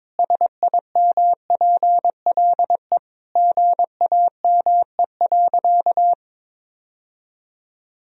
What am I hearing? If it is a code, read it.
Morse code, heard as a keyed tone, about 22 wpm, 700 Hz. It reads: SIMPLE GAME.
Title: Sentences from Top 1000 Words - Review Part 3 - 22wpm